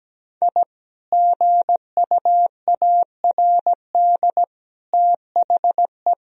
Morse code, heard as a keyed tone, about 17 wpm, 700 Hz. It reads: I GUARD THE
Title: Sentences from Top 1000 Words - Review Part 4 - 17wpm